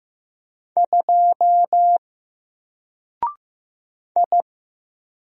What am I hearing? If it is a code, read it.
Morse code, heard as a keyed tone, about 15 wpm, 700 Hz. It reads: 2 I
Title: Single Letter-Number 15wpm